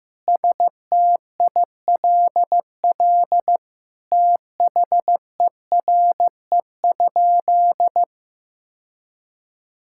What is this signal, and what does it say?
Morse code, heard as a keyed tone, about 15 wpm, 700 Hz. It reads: STILL THERE?
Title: Sentences from Top 1000 Words - Review Part 6 - 15wpm